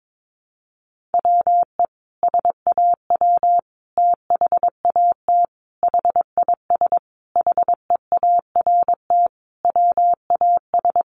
Morse code, heard as a keyed tone, about 22 wpm, 700 Hz. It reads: WE SAW THAT HIS HEART WAS
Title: Sentences from Top 1200 Words 22wpm